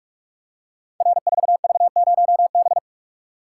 Morse code, heard as a keyed tone, about 44 wpm, 700 Hz. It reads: R4V0B
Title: Code Groups - 5 Letter:Numbers - 45wpm